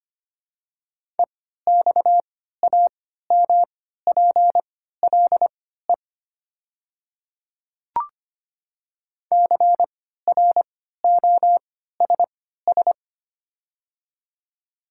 Morse code, heard as a keyed tone, about 25 wpm, spaced as for 14 wpm, 690 Hz. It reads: EXAMPLE CROSS
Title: 3x Character Spacing - Top 500 Words - 25wpm